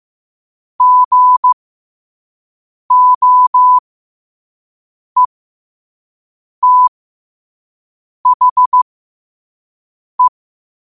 Morse code, heard as a keyed tone, about 15 wpm, spaced as for 5 wpm, 1000 Hz. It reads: GOETHE
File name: qotdfiles / 5wpm